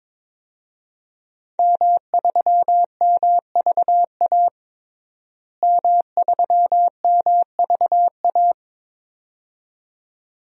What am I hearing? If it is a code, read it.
Morse code, heard as a keyed tone, about 22 wpm, 700 Hz. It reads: M3MVA M3MVA
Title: DXCC-100 Call Signs - 5 Characters - Repeated 2x - 22wpm